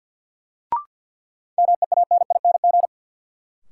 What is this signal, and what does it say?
Morse code, heard as a keyed tone, about 50 wpm, 700 Hz. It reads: MEANING
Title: Sentences from Top 900 Words - New Word Drill - 50wpm